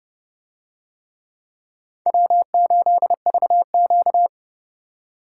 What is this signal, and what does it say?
Morse code, heard as a keyed tone, about 28 wpm, 700 Hz. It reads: W8VQ